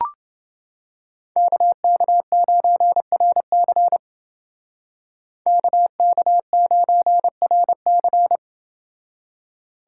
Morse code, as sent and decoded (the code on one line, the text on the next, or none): -.- -.- ----. .-. -.-. / -.- -.- ----. .-. -.-.
KK9RC KK9RC